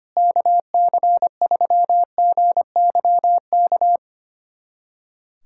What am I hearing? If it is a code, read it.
Morse code, heard as a keyed tone, about 25 wpm, 700 Hz. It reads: KC3GYK
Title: USA Callsign DE Callsign 25wpm